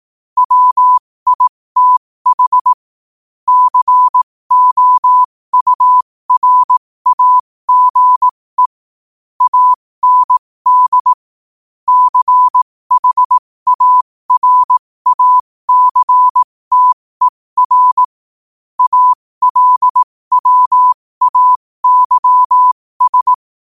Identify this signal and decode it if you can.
Morse code, heard as a keyed tone, about 18 wpm, spaced as for 15 wpm, 1000 Hz. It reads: WITH COURAGE AND CHARACTER ALWAYS
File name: qotdfiles / 15wpm